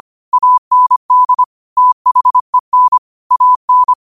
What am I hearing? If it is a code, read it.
Morse code, heard as a keyed tone, about 25 wpm, 1000 Hz. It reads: AND THEN AN
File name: qotdfiles / 25wpm